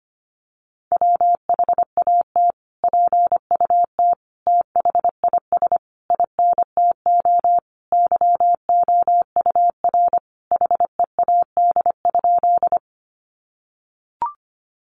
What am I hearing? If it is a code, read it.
Morse code, heard as a keyed tone, about 25 wpm, 700 Hz. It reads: WHAT PUT THIS INTO YOUR HEAD?
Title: Sentences from Top 600 Words 25wpm